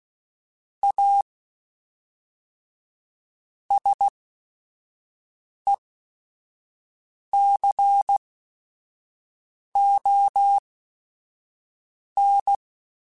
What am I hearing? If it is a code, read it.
Morse code, heard as a keyed tone, about 16 wpm, spaced as for 5 wpm, 800 Hz. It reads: A SECON
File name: Technical Subjects / Technical Subjects-mp3s / Sw Pwr Sup @5